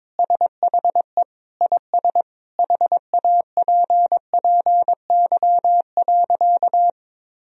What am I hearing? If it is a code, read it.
Morse code, heard as a keyed tone, about 21 wpm, 700 Hz. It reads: SHE IS HAPPY.
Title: Sentences from Top 1000 Words - Review Part 3 - 22wpm